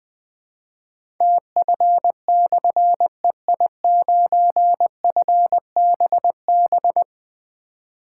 Morse code, heard as a keyed tone, about 20 wpm, 700 Hz. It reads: TF/EI9FBB
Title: Difficult Call Signs 20wpm